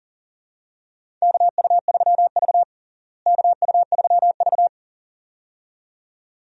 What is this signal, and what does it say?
Morse code, heard as a keyed tone, about 40 wpm, 700 Hz. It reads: KU3V KU3V
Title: USA Call Signs - 4 Characters - Repeated 2x - 40wpm